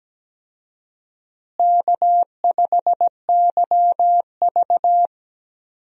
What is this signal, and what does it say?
Morse code, heard as a keyed tone, about 17 wpm, 700 Hz. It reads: K5YV
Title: USA Call Signs - 4 Characters - Repeated 2x - 17wpm